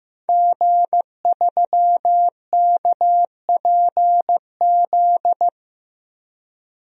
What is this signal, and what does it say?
Morse code, heard as a keyed tone, about 15 wpm, 700 Hz. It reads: G3KPZ
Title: DXCC-100 Call Signs - 5 Characters - 15wpm